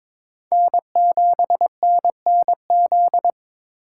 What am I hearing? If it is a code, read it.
Morse code, heard as a keyed tone, about 22 wpm, 700 Hz. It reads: N7NNZ
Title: USA Call Signs - 3x Character Spacing - 22wpm